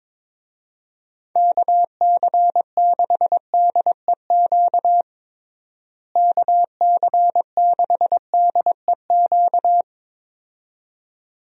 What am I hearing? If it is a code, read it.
Morse code, heard as a keyed tone, about 22 wpm, 700 Hz. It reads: KC6DEQ KC6DEQ
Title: USA Call Signs - 6 Characters - Repeated 2x - 22wpm